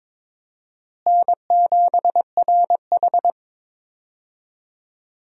Morse code, heard as a keyed tone, about 21 wpm, 700 Hz. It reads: N7RH
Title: USA Call Signs - 4 Characters - Repeated 3x - 22wpm